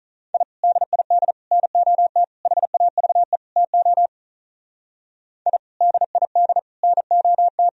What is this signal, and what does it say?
Morse code, heard as a keyed tone, about 44 wpm, 700 Hz. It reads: I DID NOT HAVE TO I DID NOT